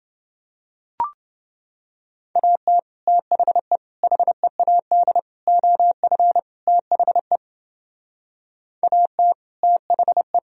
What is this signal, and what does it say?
Morse code, heard as a keyed tone, about 29 wpm, 700 Hz. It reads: AT THE HEAD OF THE AT THE